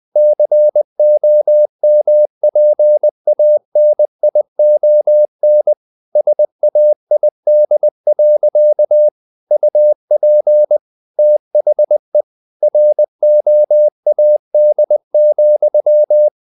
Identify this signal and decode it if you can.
Morse code, heard as a keyed tone, about 20 wpm, 600 Hz. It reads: COMPANION SAID. UP THE ROAD,
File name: the old man and the sea part 2 20wpm